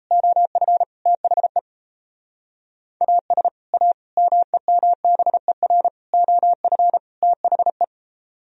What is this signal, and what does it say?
Morse code, heard as a keyed tone, about 34 wpm, 700 Hz. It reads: OF THE AS A MEMBER OF THE
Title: Speed-Racing - Sets of 5 Words 28wpm